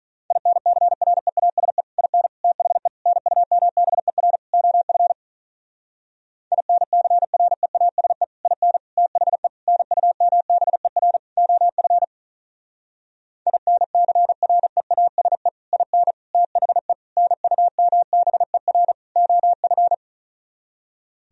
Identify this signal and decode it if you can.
Morse code, heard as a keyed tone, about 43 wpm, 700 Hz. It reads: INCREASE IN THE NUMBER OF INCREASE IN THE NUMBER OF INCREASE IN THE NUMBER OF
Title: Speed-Racing - Sets of 5 Words 35wpm